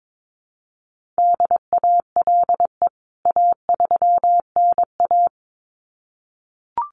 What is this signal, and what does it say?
Morse code, heard as a keyed tone, about 22 wpm, 700 Hz. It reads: DALE A3NA
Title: CWT Exchange 22wpm